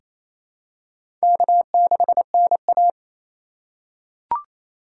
Morse code, heard as a keyed tone, about 28 wpm, 700 Hz. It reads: K6NA
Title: K1USN SST QSO Practice - 28wpm